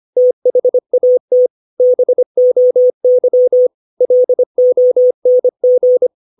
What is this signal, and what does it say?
Morse code, heard as a keyed tone, about 25 wpm, 500 Hz. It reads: THAT BOY LONG
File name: 632 words / 632 Words 25WPM